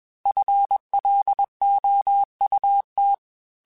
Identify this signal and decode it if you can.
Morse code, heard as a keyed tone, about 21 wpm, 800 Hz. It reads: FLOUT
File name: FL Runs / FL Runs mp3s / FL Runs @20